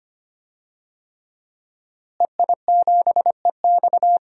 Morse code, heard as a keyed tone, about 25 wpm, 700 Hz. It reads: EI7EX
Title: DXCC-100 Call Signs - 5 Characters - Repeated 3x - 25wpm